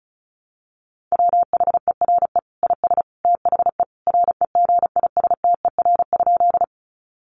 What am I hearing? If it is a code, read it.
Morse code, heard as a keyed tone, about 34 wpm, 700 Hz. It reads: WHERE IS THE REGISTER?
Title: Sentences from Top 1200 Words 35wpm